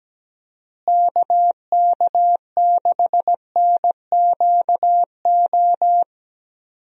Morse code, heard as a keyed tone, about 17 wpm, 700 Hz. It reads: KK6NQO
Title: USA Call Signs - 6 Characters - Repeated 2x - 17wpm